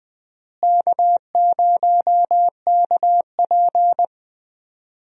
Morse code, heard as a keyed tone, about 21 wpm, 700 Hz. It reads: K0KP
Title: USA Call Signs - 4 Characters - Repeated 3x - 20wpm